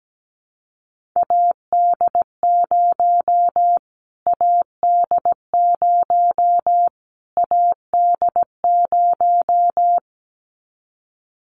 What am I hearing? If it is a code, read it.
Morse code, heard as a keyed tone, about 17 wpm, 700 Hz. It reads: AD0 AD0 AD0
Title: Callsign Prefixes for Top 25 DXCC - Repeated 3x - 17wpm